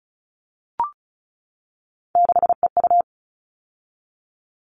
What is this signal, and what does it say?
Morse code, heard as a keyed tone, about 34 wpm, 700 Hz. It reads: 6EU
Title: All Cut Numbers - 3-Digit 35wpm